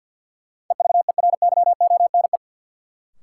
Morse code, heard as a keyed tone, about 53 wpm, 700 Hz. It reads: EVERYONE
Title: Sentences from Top 800 Words - New Word Drill - 50wpm